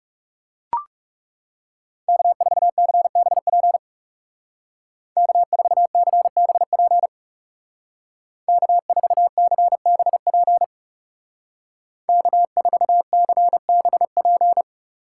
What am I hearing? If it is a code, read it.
Morse code, heard as a keyed tone, about 35 wpm, 700 Hz. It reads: K4CBP K4CBP K4CBP K4CBP
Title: Speed Racing - USA 1x3 Call Signs 30wpm